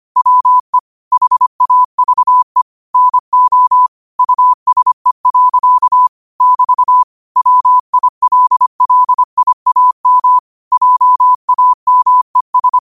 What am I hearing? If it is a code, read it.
Morse code, heard as a keyed tone, about 25 wpm, 1000 Hz. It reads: WE HAVE NO USE. <BT> WILLIAM JAMES
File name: qotdfiles / 25wpm